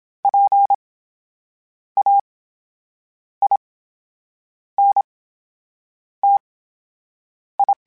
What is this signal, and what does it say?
Morse code, heard as a keyed tone, about 27 wpm, spaced as for 7 wpm, 800 Hz. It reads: PAINTI
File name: Special INGs / Special INGs-mp3s / VerySlowToFaster Ing @25